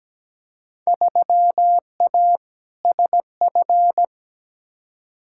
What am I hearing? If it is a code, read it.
Morse code, heard as a keyed tone, about 17 wpm, 700 Hz. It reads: 3A SF
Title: Field Day Exchanges Encore - 17wpm